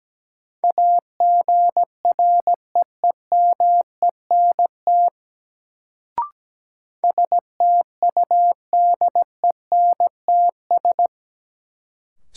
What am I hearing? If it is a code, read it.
Morse code, heard as a keyed tone, about 17 wpm, 700 Hz. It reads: AGREEMENT STUDENTS
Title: Sets of 5 Words Encore - New Word Drill - 17wpm